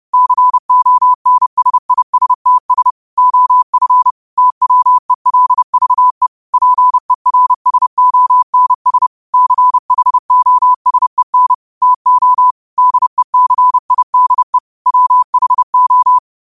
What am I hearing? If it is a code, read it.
Morse code, heard as a keyed tone, about 30 wpm, 1000 Hz. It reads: CONSISTS OF TWELVE PERSONS CHOSEN TO DECIDE WHO
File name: qotdfiles / 30wpm